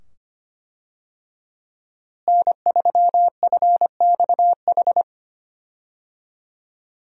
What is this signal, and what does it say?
Morse code, heard as a keyed tone, about 25 wpm, 700 Hz. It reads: N3FXH
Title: USA Call Signs - 5 Characters - Repeated 2x - 25wpm